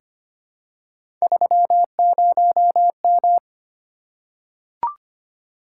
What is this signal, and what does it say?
Morse code, heard as a keyed tone, about 25 wpm, 700 Hz. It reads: 30M
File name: Three Letter-Number Combinations - 25wpm